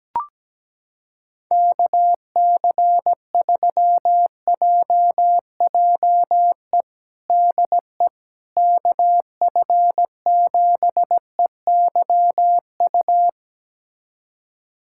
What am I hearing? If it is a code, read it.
Morse code, heard as a keyed tone, about 17 wpm, 700 Hz. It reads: KC3JJE DE KF7EYU
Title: USA Callsign DE Callsign 17wpm